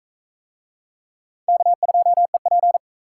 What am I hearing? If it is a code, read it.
Morse code, heard as a keyed tone, about 37 wpm, 700 Hz. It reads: K2EP